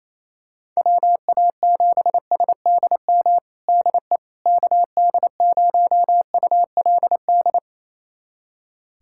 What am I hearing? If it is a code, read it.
Morse code, heard as a keyed tone, about 28 wpm, 700 Hz. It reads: WA7SDM DE KD0ULD
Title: USA Callsign DE Callsign - 6 Characters - 28wpm